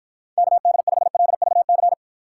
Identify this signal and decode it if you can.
Morse code, heard as a keyed tone, about 52 wpm, 700 Hz. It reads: KD5LVC